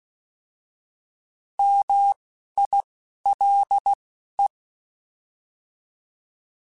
Morse code, heard as a keyed tone, about 16 wpm, spaced as for 11 wpm, 800 Hz. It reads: MILE
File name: Alphabetical / Alphabetical mp3s / Ms @10